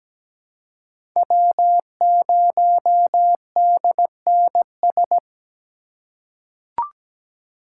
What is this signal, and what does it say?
Morse code, heard as a keyed tone, about 17 wpm, 700 Hz. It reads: W0DNS
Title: USA Call Signs - 8x Character Spacing - 17wpm